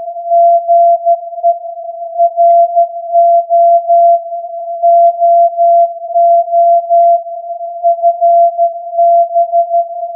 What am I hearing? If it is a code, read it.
Morse code, heard as a keyed tone, about 13 wpm, 680 Hz. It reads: GE RO OO FB